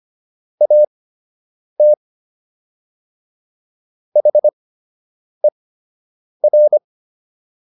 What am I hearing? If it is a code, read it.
Morse code, heard as a keyed tone, about 25 wpm, spaced as for 8 wpm, 600 Hz. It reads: AT HER